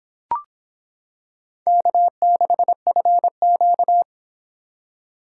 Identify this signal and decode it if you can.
Morse code, heard as a keyed tone, about 26 wpm, 700 Hz. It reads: K6FQ